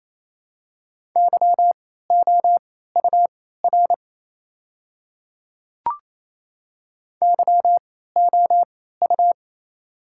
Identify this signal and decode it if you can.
Morse code, heard as a keyed tone, about 28 wpm, spaced as for 16 wpm, 700 Hz. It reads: YOUR YOU
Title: 3x Character Spacing - Top 100 Words - 28wpm